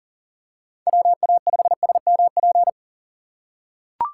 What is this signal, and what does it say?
Morse code, heard as a keyed tone, about 38 wpm, 700 Hz. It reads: WA5SMP
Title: USA Call Signs - 6 Characters - 3x Character Spacing - 40wpm